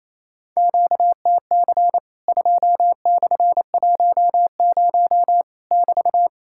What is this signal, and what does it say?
Morse code, heard as a keyed tone, about 28 wpm, 700 Hz. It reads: QTC 2/10 <BT>
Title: Worked All Europe DX Contest - QTC Exchange - 28wpm